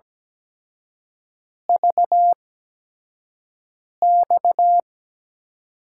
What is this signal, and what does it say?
Morse code, heard as a keyed tone, about 17 wpm, spaced as for 5 wpm, 700 Hz. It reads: VX
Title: USA Call Signs - 5 Characters - 8x Character Spacing - 17wpm